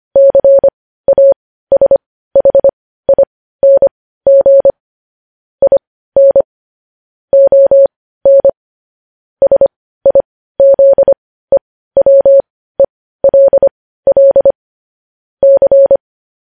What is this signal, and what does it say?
Morse code, heard as a keyed tone, about 25 wpm, spaced as for 15 wpm, 550 Hz. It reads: CASHING IN ON SIZEWELL C
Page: news-headlines-in-morse-code-at-15-wpm